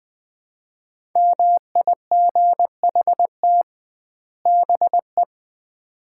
Morse code, heard as a keyed tone, about 20 wpm, 700 Hz. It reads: MIGHT BE
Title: 2x Word-Spacing - Sets of 2 Words - 20wpm